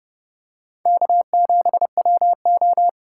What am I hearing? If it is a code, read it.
Morse code, heard as a keyed tone, about 30 wpm, 700 Hz. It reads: K7WO
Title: USA Call Signs - 4 Characters - 4x Character Spacing - 30wpm